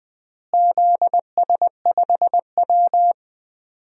Morse code, heard as a keyed tone, about 20 wpm, 700 Hz. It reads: ZS5W
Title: DXCC-100 Call Signs - 4 Characters - Repeated 2x - 20wpm